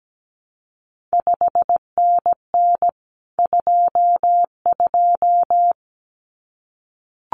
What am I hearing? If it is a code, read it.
Morse code, heard as a keyed tone, about 17 wpm, 700 Hz. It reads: 5NN 22
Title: CQ WW DX Contest Exchange Encore - 17wpm